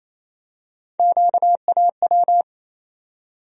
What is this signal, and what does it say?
Morse code, heard as a keyed tone, about 28 wpm, 700 Hz. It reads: QAW
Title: USA Call Sign Postfixes - 3 Letters 28wpm